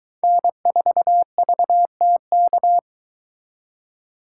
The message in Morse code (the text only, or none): N4VTK